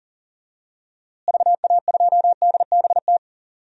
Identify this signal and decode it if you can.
Morse code, heard as a keyed tone, about 40 wpm, 700 Hz. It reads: VA2DBT
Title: DXCC-100 Call Signs - 6 Characters - Repeated 2x - 40wpm